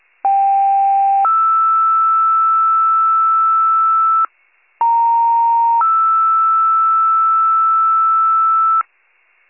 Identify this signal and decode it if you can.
Two-tone page, heard as a one-second tone, then a three-second tone, about 790 then 1320 Hz, about 930 then 1320 Hz, with noise. It is heard from 0.2 to 8.8 s.